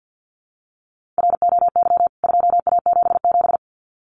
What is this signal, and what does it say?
Morse code, heard as a keyed tone, about 48 wpm, 700 Hz. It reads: ROY 2A77
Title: CWT Exchange 50wpm